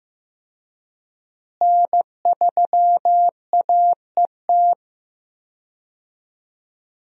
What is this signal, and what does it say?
Morse code, heard as a keyed tone, about 14 wpm, 700 Hz. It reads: N3AET
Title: USA Call Signs - 5 Characters - Repeated 2x - 15wpm